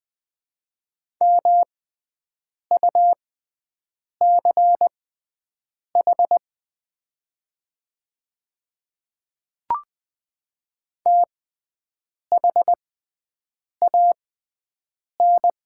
Morse code, heard as a keyed tone, about 20 wpm, spaced as for 7 wpm, 700 Hz. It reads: MUCH THAN
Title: 6x Character Spacing - Top 200 Words - 20wpm